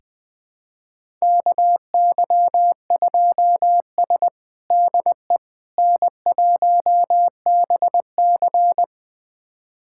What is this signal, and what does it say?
Morse code, heard as a keyed tone, about 20 wpm, 700 Hz. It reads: KY2S DE N1BC
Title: USA Callsign DE Callsign - 4 Characters - 20wpm